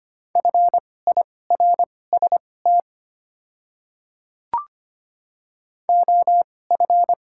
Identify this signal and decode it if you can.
Morse code, heard as a keyed tone, about 25 wpm, spaced as for 18 wpm, 700 Hz. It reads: FIRST OF
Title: 2x Character Spacing - Top 200 Words - 25wpm